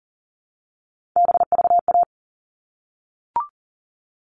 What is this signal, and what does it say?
Morse code, heard as a keyed tone, about 40 wpm, 700 Hz. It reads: BVA
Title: All Cut Numbers - 3-Digit 40wpm